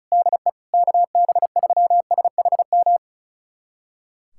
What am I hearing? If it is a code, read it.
Morse code, heard as a keyed tone, about 36 wpm, 700 Hz. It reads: DE KB3SHM
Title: USA Callsign DE Callsign - 6 Characters - 35wpm